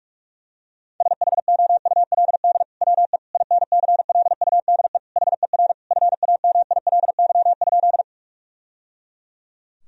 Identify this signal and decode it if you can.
Morse code, heard as a keyed tone, about 46 wpm, 700 Hz. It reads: SHOULD WE INCLUDE HER FAMILY?
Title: Sentences from Top 1000 Words - Review Part 3 - 45wpm